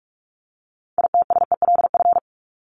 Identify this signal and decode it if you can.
Morse code, heard as a keyed tone, about 44 wpm, 700 Hz. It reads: ITSELF
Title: Top 501-750 Words 45wpm